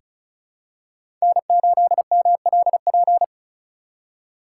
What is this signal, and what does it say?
Morse code, heard as a keyed tone, about 36 wpm, 700 Hz. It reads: N8MLP
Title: USA Call Signs - 5 Characters - 3x Character Spacing - 35wpm